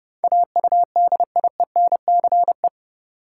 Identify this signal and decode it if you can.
Morse code, heard as a keyed tone, about 30 wpm, 700 Hz. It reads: AUDIENCE
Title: Sentences from Top 800 Words - New Word Drill - 30wpm